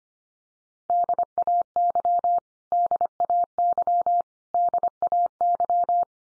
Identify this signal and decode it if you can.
Morse code, heard as a keyed tone, about 25 wpm, 700 Hz. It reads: DAY DAY DAY
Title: Top 100 Words 25wpm x3